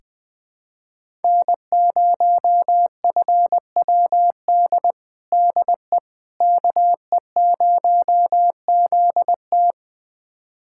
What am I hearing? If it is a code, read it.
Morse code, heard as a keyed tone, about 20 wpm, 700 Hz. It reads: N0FWD DE KE0ZT
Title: USA Callsign DE Callsign - 5 Characters - 20wpm